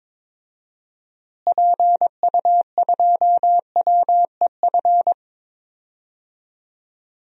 Morse code, heard as a keyed tone, about 22 wpm, 700 Hz. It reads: PU2WEF
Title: DXCC-100 Call Signs - 6 Characters - Repeated 3x - 22wpm